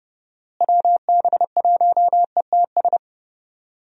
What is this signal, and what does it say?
Morse code, heard as a keyed tone, about 31 wpm, 700 Hz. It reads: WB1ETS
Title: USA Call Signs - 6 Characters - Repeated 2x - 30wpm